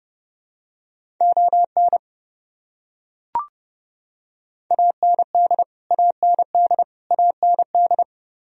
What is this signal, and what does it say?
Morse code, heard as a keyed tone, about 30 wpm, 700 Hz. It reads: ON AND AND AND
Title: Top 100 Words Repeated 30wpm